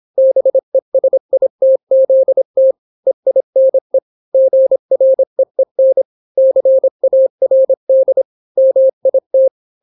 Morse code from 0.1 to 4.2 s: BESITZT EINE